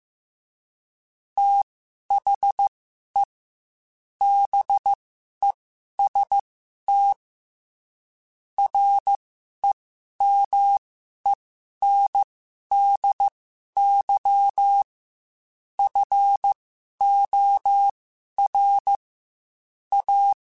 Morse code, heard as a keyed tone, about 15 wpm, spaced as for 11 wpm, 800 Hz. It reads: THE BEST REMENDY FOR A
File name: Quotes / Quotes-mp3s / Temper @13